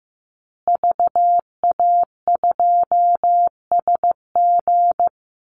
Morse code, heard as a keyed tone, about 15 wpm, 700 Hz. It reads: VA2SG
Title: Call Signs Repeated 2x - 15wpm